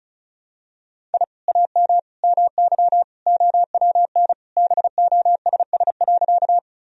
Morse code, heard as a keyed tone, about 35 wpm, 700 Hz. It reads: I AM MY OWN BOSS.
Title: Sentences from Top 1000 Words - Review Part 4 - 35wpm